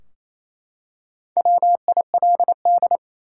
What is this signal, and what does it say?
Morse code, heard as a keyed tone, about 28 wpm, 700 Hz. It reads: WILD
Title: Four Letter Words - 28wpm